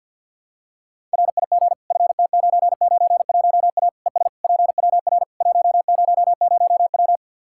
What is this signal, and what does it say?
Morse code, heard as a keyed tone, about 46 wpm, 700 Hz. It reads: RIG FT991A ES PWR 100W
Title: Practice QSO - Exchange 2 - Info - 50wpm